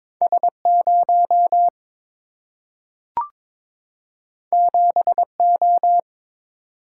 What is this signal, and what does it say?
Morse code, heard as a keyed tone, about 22 wpm, 700 Hz. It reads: S 0 7 O
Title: Single-Letter Single-Number 22wpm